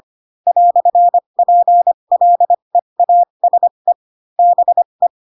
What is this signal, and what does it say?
Morse code, heard as a keyed tone, about 25 wpm, spaced as for 22 wpm, 700 Hz. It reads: "PLEASE BE